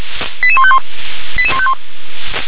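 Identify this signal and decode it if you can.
Five-tone selective call (ZVEI), heard as quick runs of five tones, tones of about 70 ms. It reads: 88151 88151
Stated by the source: ZVEI five-tone call received by SDR.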